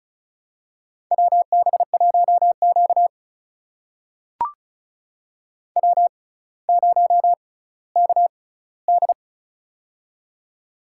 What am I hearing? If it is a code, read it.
Morse code, heard as a keyed tone, about 35 wpm, spaced as for 18 wpm, 700 Hz. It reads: WB1Q W0KD
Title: USA Call Signs - 6x Character Spacing - 35wpm